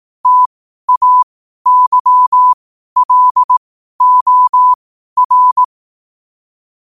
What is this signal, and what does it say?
Morse code, heard as a keyed tone, about 18 wpm, spaced as for 12 wpm, 1000 Hz. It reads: TAYLOR
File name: qotdfiles / 12wpm